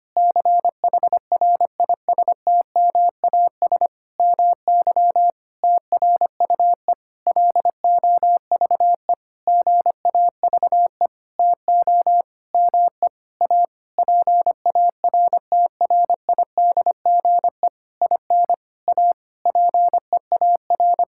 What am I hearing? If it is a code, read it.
Morse code, heard as a keyed tone, about 25 wpm, 700 Hz. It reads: CHRISTMAS MY TRUE LOVE GAVE TO ME A PARTRIDGE IN A PEAR
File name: practice / 25wpm / Sentences with 9-Word Phrases - 25wpm